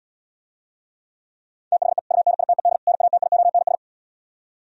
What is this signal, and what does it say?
Morse code, heard as a keyed tone, about 77 wpm, 700 Hz. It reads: THE UNITED NATIONS